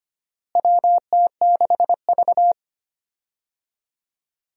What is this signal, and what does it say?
Morse code, heard as a keyed tone, about 25 wpm, 700 Hz. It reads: WT6V
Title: USA Call Signs - 4 Characters - Repeated 3x - 25wpm